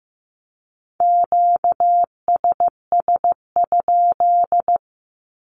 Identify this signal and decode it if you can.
Morse code, heard as a keyed tone, about 15 wpm, 700 Hz. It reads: QSS?
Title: Q-Codes Repeated 3x 15wpm